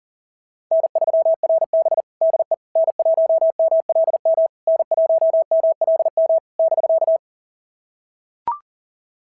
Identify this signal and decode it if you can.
Morse code, heard as a keyed tone, about 41 wpm, 690 Hz. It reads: N3RB DE N1MLM N1MLM <BK>
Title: SOTA QSO Practice 40wpm